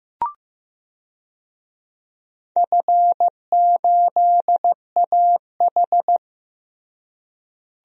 Morse code, heard as a keyed tone, about 15 wpm, 700 Hz. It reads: F8AH
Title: DXCC-100 Call Signs - 4 Characters - Repeated 3x - 15wpm